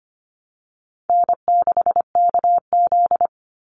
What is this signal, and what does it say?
Morse code, heard as a keyed tone, about 25 wpm, 700 Hz. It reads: N6KZ
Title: NA SOTA Chaser Callsigns 25wpm